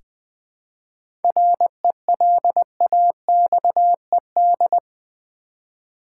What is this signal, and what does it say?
Morse code, heard as a keyed tone, about 21 wpm, 700 Hz. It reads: RELAXED
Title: 7 Letter Words - First 3 Letters Given - 20wpm